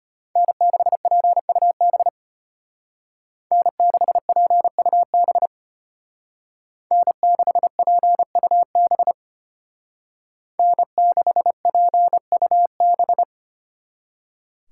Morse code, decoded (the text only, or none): N6PUB N6PUB N6PUB N6PUB